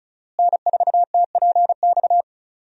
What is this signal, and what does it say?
Morse code, heard as a keyed tone, about 36 wpm, 700 Hz. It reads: N4TPX
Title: USA Call Signs - 5 Characters - Repeated 3x - 35wpm